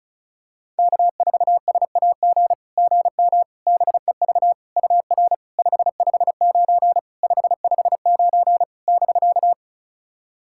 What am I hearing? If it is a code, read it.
Morse code, heard as a keyed tone, about 36 wpm, 690 Hz. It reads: K4SAG GM BEV UR 559 559 <BK>
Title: SOTA QSO Practice 35wpm